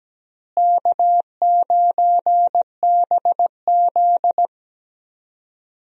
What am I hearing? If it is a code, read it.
Morse code, heard as a keyed tone, about 17 wpm, 700 Hz. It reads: K9BZ
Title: USA Call Signs - 4 Characters - Repeated 3x - 17wpm